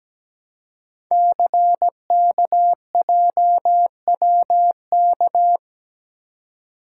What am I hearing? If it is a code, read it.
Morse code, heard as a keyed tone, about 17 wpm, 700 Hz. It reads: CKJWK
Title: Code Groups - 5 Letters - 17wpm